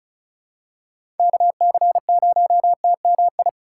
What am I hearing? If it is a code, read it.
Morse code, heard as a keyed tone, about 35 wpm, 700 Hz. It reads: KC0TMI
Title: USA Call Signs - 6 Characters - 3x Character Spacing - 35wpm